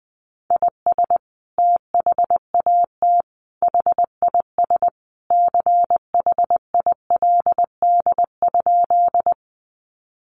Morse code, heard as a keyed tone, about 20 wpm, 700 Hz. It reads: IS THAT HIS CHILD?
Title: Sentences from Top 800 Words 20wpm